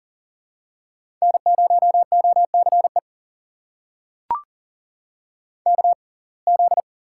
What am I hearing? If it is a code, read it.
Morse code, heard as a keyed tone, about 40 wpm, spaced as for 29 wpm, 700 Hz. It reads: N0OCE KZ
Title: USA Call Signs - 6x Character Spacing - 40wpm